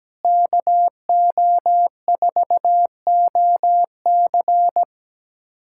Morse code, heard as a keyed tone, about 17 wpm, 700 Hz. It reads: KO4OC
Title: USA Callsign DE Callsign - 5 Characters - 17wpm